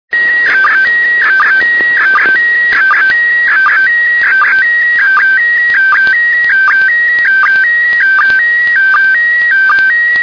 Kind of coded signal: SSTV (slow-scan television)